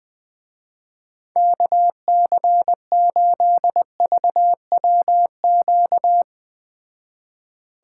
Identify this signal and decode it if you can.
Morse code, heard as a keyed tone, about 20 wpm, 700 Hz. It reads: KC8VWQ